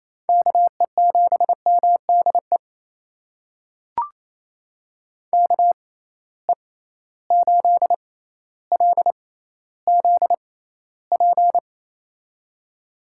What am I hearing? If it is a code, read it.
Morse code, heard as a keyed tone, about 28 wpm, spaced as for 14 wpm, 700 Hz. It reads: KE7MDE KE8LZP
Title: USA Call Signs - 6 Characters - 6x Character Spacing - 28wpm